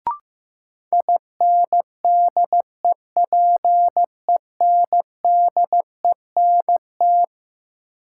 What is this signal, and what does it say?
Morse code, heard as a keyed tone, about 15 wpm, 700 Hz. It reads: INDEPENDENT
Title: Sentences from Top 900 Words - New Word Drill - 15wpm